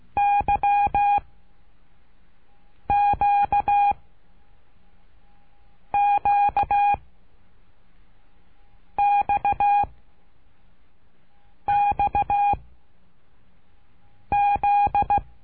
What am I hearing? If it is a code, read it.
Morse code, heard as a keyed tone, about 16 wpm, 810 Hz. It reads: Y Q Q X X Z
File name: cursos / CURSO DE TELEGRAFIA / Telegrafia - Trilha 12